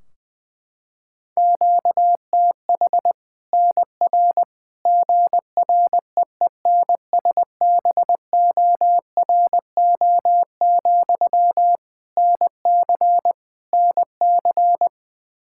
Morse code, heard as a keyed tone, about 20 wpm, 700 Hz. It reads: QTH NR GREENSBORO, NC NC
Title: Practice QSO - Exchange 1 - Beginning - 20wpm